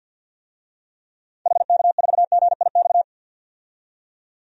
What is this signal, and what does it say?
Morse code, heard as a keyed tone, about 49 wpm, 700 Hz. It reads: HK4GIX
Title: DXCC-100 Call Signs - 6 Characters - 50wpm